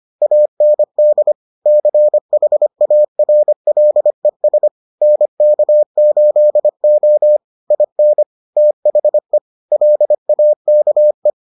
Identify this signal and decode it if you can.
Morse code, heard as a keyed tone, about 25 wpm, 600 Hz. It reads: AND CHARLES NK8O IN THE LAKE